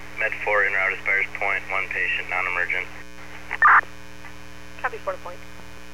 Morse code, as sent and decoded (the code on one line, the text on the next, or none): none